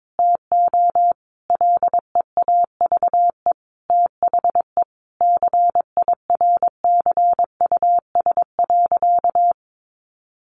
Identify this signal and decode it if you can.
Morse code, heard as a keyed tone, about 22 wpm, 700 Hz. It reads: TO LEAVE THE CIRCUS.
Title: Easy Phrases 22wpm